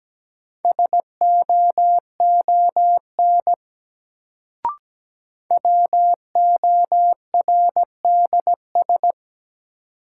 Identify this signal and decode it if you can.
Morse code, heard as a keyed tone, about 17 wpm, 700 Hz. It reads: SOON WORDS
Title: Sets of 3 Words - New Word Drill - 17wpm